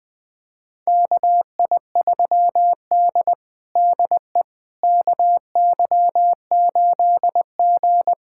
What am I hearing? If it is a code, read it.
Morse code, heard as a keyed tone, about 21 wpm, 700 Hz. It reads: KI3D DE KY8G
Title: USA Callsign DE Callsign - 4 Characters - 20wpm